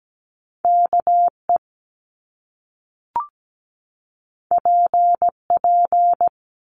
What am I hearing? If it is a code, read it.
Morse code, heard as a keyed tone, about 17 wpm, 700 Hz. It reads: KE PP
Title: Top Two Letter Combinations in Words 17wpm